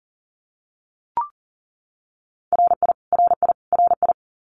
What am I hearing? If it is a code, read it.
Morse code, heard as a keyed tone, about 40 wpm, 700 Hz. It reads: RI RI RI
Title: US States Repeated 3x 40wpm